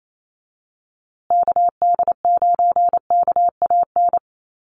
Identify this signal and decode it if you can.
Morse code, heard as a keyed tone, about 28 wpm, 700 Hz. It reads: KD9KAN
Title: USA Call Signs Repeated 3x 28wpm